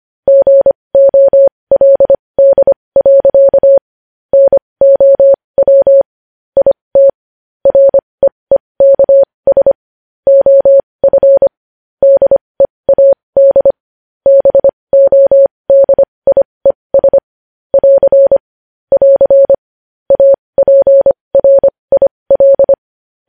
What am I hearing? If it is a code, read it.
Morse code, heard as a keyed tone, about 25 wpm, spaced as for 20 wpm, 550 Hz. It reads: GOLD. NOW IT REEKS OF DEAD BODIES <AR> <AR> APRIL
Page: news-headlines-in-morse-code-at-20-wpm